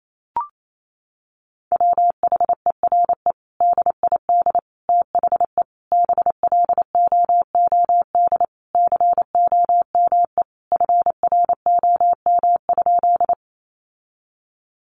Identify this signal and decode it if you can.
Morse code, heard as a keyed tone, about 28 wpm, 700 Hz. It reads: WHERE DID THE BLOOD COME FROM?
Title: Sentences from Top 1100 Words 28wpm